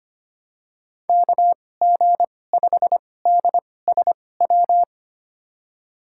Morse code, heard as a keyed tone, about 25 wpm, spaced as for 18 wpm, 700 Hz. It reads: KG5DSW
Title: USA Call Signs - 6 Characters - 2x Character Spacing - 25wpm